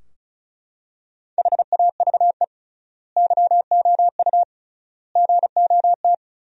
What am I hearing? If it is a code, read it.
Morse code, heard as a keyed tone, about 34 wpm, 700 Hz. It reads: HAVE YOU GOT
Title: 3x Word-Spacing - Sentences from Top 300 Words - 35wpm